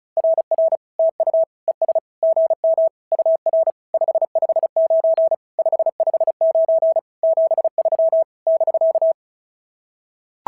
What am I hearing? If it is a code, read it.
Morse code, heard as a keyed tone, about 36 wpm, 650 Hz. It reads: RR TU ES GM UR 559 559 73 <BK>
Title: SOTA QSO Practice 35wpm